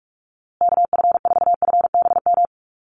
Morse code, heard as a keyed tone, about 44 wpm, 700 Hz. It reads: KF4FBM
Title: USA Call Signs Repeated 2x 45wpm